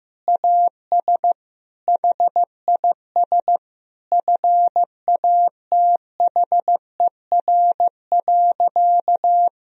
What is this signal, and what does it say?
Morse code, heard as a keyed tone, about 16 wpm, 700 Hz. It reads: AS HIS FATHER.
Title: Sentences from Top 1000 Words - Review Part 4 - 15wpm